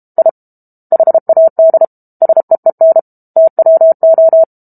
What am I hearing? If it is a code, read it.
Morse code, heard as a keyed tone, about 32 wpm, 650 Hz. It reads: I HAD SEEN TWO
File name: Book Two / 06-Chapter 6 - 32 WPM